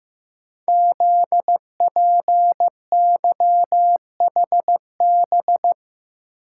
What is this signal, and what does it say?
Morse code, heard as a keyed tone, about 15 wpm, 700 Hz. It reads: ZPYHB
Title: Code Groups - 5 Letter:Numbers - 15wpm